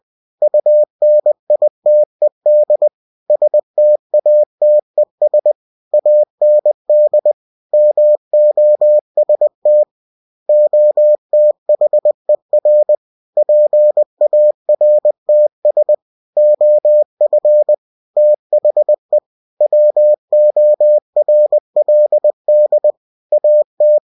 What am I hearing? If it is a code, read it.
Morse code, heard as a keyed tone, about 20 wpm, 600 Hz. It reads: UNITED STATES AND MOST OTHER PARTS OF THE WORLD AT
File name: cw-books / winnie-the-pooh / winnie-20wpm-page-0115